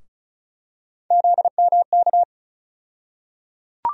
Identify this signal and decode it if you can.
Morse code, heard as a keyed tone, about 36 wpm, 700 Hz. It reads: ZMK
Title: USA Call Sign Postfixes - 3 Letters 35wpm